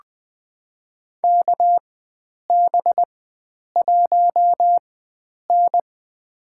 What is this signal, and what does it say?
Morse code, heard as a keyed tone, about 21 wpm, spaced as for 9 wpm, 700 Hz. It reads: KB1N
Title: USA Call Signs - 4x Character Spacing - 20wpm